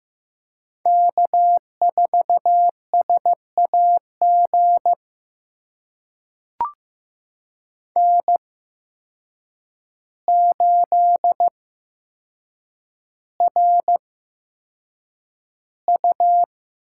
Morse code, heard as a keyed tone, about 15 wpm, spaced as for 7 wpm, 700 Hz. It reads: K4SAG N8RU
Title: USA Call Signs - 5 Characters - 8x Character Spacing - 15wpm